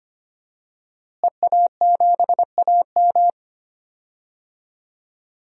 Morse code, heard as a keyed tone, about 25 wpm, 700 Hz. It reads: EA7AM
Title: DXCC-100 Call Signs - 5 Characters - Repeated 2x - 25wpm